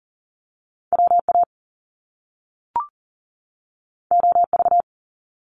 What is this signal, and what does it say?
Morse code, heard as a keyed tone, about 42 wpm, 700 Hz. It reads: WA OV
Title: Top Two Letter Combinations in Words 40wpm